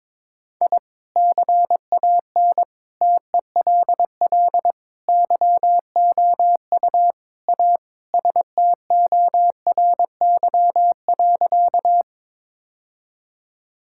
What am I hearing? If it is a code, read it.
Morse code, heard as a keyed tone, about 22 wpm, 700 Hz. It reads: I CAN TELL YOU A STORY.
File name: practice / 22wpm / Sentences with 4-Word Phrases - 22wpm